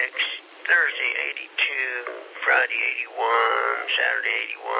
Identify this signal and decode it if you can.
Emergency Alert System (SAME) tones, heard as harsh buzzing bursts, with speech. It is heard from 0.5 to 4.4 s.